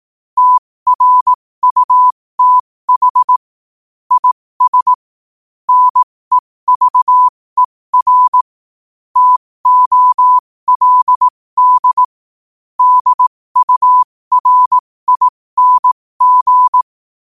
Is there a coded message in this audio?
Yes: Morse code from 0.4 to 16.8 s, heard as a keyed tone, about 18 wpm, spaced as for 15 wpm, 1000 Hz. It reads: TRUTH IS NEVER TOLD DURING